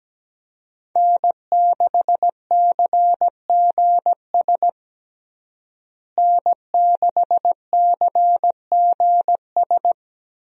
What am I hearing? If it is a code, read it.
Morse code, heard as a keyed tone, about 17 wpm, 700 Hz. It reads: N6CGS N6CGS
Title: USA Call Signs - 5 Characters - Repeated 3x - 17wpm